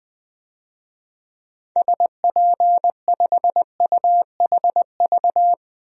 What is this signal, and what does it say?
Morse code, heard as a keyed tone, about 20 wpm, 700 Hz. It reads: SP5UHV
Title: DXCC-100 Call Signs - 6 Characters - 20wpm